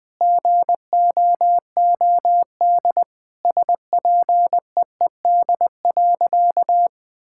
Morse code, heard as a keyed tone, about 20 wpm, 700 Hz. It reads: GOOD SPEED.
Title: Sentences from Top 1000 Words - Review Part 5 - 20wpm